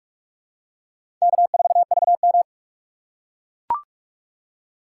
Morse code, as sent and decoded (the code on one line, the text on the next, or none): -.- ....- ...- --
K4VM